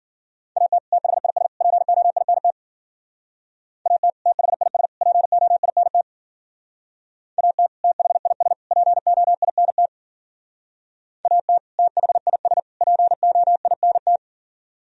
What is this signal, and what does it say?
Morse code, heard as a keyed tone, about 52 wpm, 700 Hz. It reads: AT THIS POINT AT THIS POINT AT THIS POINT AT THIS POINT